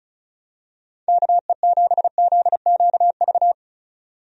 Morse code, heard as a keyed tone, about 36 wpm, 700 Hz. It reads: KE7ZQV
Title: USA Call Signs - 6 Characters - 35wpm